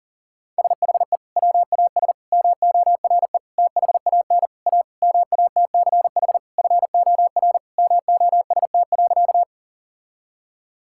Morse code, heard as a keyed tone, about 40 wpm, 700 Hz. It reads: SHE WAS MORE THAN A MATCH FOR MOST.
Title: Sentences from Top 1600 Words 40wpm